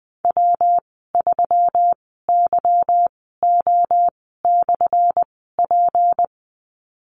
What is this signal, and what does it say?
Morse code, heard as a keyed tone, about 20 wpm, spaced as for 14 wpm, 700 Hz. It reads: W3YO/P
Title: 179 Lesson 29g - Slash - Call Signs 20wpm